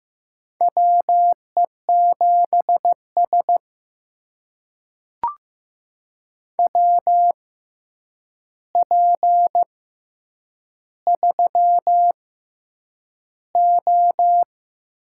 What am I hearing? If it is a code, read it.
Morse code, heard as a keyed tone, about 15 wpm, spaced as for 8 wpm, 700 Hz. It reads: WE7S WP3O
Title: USA Call Signs - 6x Character Spacing - 15wpm